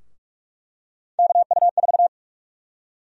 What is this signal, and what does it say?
Morse code, heard as a keyed tone, about 46 wpm, 700 Hz. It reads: KU4
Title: USA Call Sign Prefixes - 45wpm